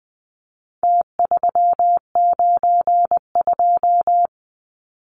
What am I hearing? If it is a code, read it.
Morse code, heard as a keyed tone, about 20 wpm, 700 Hz. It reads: T392
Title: Numbers - Contest Style - 4-Digit 20wpm